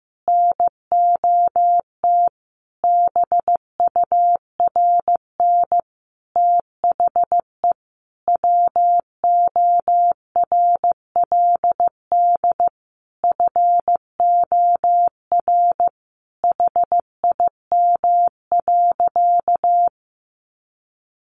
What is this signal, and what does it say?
Morse code, heard as a keyed tone, about 15 wpm, 700 Hz. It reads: NOT BURN THE WORLD FOR HIM.
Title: Sentences from Top 1300 Words 15wpm